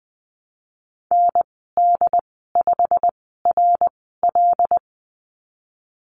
Morse code, heard as a keyed tone, about 20 wpm, spaced as for 15 wpm, 700 Hz. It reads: ND5RL
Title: 088 Lesson 15g - Number 5 - Call Signs 20wpm